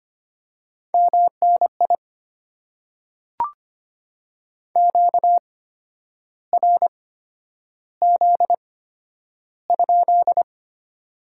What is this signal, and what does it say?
Morse code, heard as a keyed tone, about 25 wpm, spaced as for 10 wpm, 700 Hz. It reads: MNI QRZ?
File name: Top 90 QSO Elements - 8x Character Spacing - 25wpm